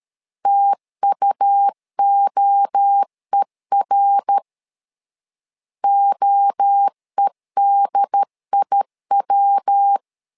Morse code, heard as a keyed tone, about 13 wpm, 800 Hz. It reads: TUOER OEDIW
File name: LEZIONI / 3 MWRO / 3 MWRO 12wpm 1730